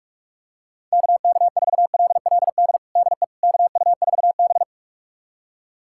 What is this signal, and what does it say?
Morse code, heard as a keyed tone, about 44 wpm, 700 Hz. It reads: KK4LLD DE KU4B
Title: USA Callsign DE Callsign 45wpm